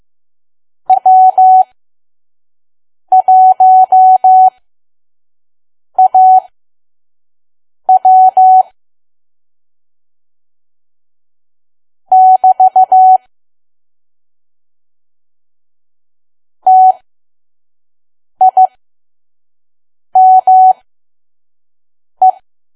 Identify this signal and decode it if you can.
Morse code, heard as a keyed tone, about 15 wpm, spaced as for 5 wpm, 750 Hz. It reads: W1AW <BT> TIME
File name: Archive / 5 WPM files / PM